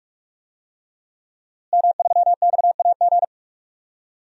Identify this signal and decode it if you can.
Morse code, heard as a keyed tone, about 44 wpm, 700 Hz. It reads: M3XAG
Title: DXCC-100 Call Signs - 5 Characters - Repeated 3x - 45wpm